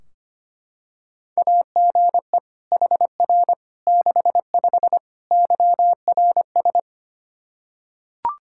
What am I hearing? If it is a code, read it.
Morse code, heard as a keyed tone, about 25 wpm, 700 Hz. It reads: AGE HR 65 YRS
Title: Practice QSO - Exchange 3 - More Info - 25wpm